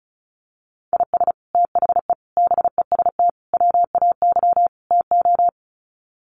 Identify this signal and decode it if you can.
Morse code, heard as a keyed tone, about 34 wpm, 700 Hz. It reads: IS THE BEST WAY TO
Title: Sets of 5 Words Encore 35wpm